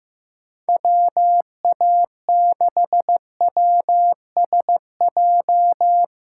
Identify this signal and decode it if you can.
Morse code, heard as a keyed tone, about 15 wpm, 700 Hz. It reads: WA6WSJ
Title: USA Callsign DE Callsign - 6 Characters - 15wpm